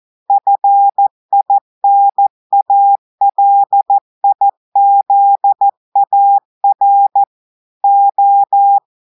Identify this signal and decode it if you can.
Morse code, heard as a keyed tone, about 14 wpm, 800 Hz. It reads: FINALIZAR O
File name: audios / portugais / portugais texte en clair 2 vitesse 14 mots minutes